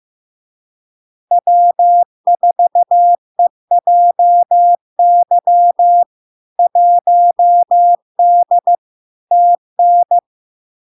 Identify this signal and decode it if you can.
Morse code, heard as a keyed tone, about 15 wpm, 700 Hz. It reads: W4EJY 1D TN
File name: LARK-FD-2022-Log-in-CW-15wpm-700-Hz